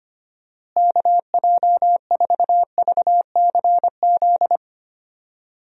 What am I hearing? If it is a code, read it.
Morse code, heard as a keyed tone, about 25 wpm, 700 Hz. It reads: KJ4VCZ
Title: USA Call Signs - 6 Characters - Repeated 2x - 25wpm